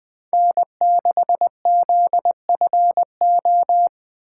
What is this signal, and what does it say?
Morse code, heard as a keyed tone, about 20 wpm, 700 Hz. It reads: N6ZFO